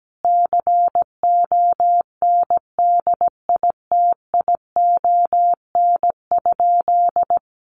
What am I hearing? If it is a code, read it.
Morse code, heard as a keyed tone, about 17 wpm, 700 Hz. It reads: CONDITION?
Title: Sentences from Top 1000 Words 17wpm